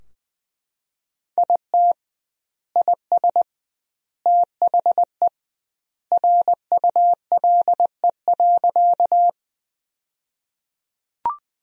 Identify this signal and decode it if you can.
Morse code, heard as a keyed tone, about 21 wpm, 700 Hz. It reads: IT IS THE RULE.
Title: 2x Word-Spacing - Sentences from Top 400 Words - 20wpm